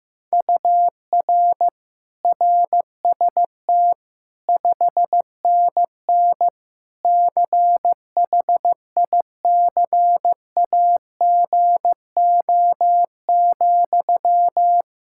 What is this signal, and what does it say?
Morse code, heard as a keyed tone, about 15 wpm, 700 Hz. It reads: UR RST 5NN CHICAGO,
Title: K1USN SST QSO Practice 15wpm